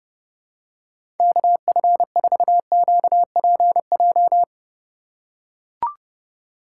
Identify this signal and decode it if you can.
Morse code, heard as a keyed tone, about 30 wpm, 700 Hz. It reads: KF4QPJ
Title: USA Call Signs - 6 Characters - 4x Character Spacing - 30wpm